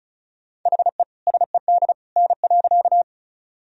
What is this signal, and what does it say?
Morse code, heard as a keyed tone, about 36 wpm, 700 Hz. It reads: HE SED N.
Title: Sentences from Top 100 Words and Abbreviations 35wpm